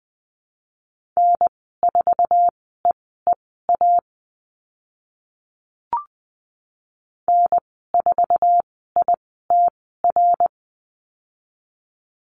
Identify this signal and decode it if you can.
Morse code, heard as a keyed tone, about 20 wpm, spaced as for 14 wpm, 700 Hz. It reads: N4EEA N4ITR
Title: 062 Lesson 11g - Letter h - Call Signs 20wpm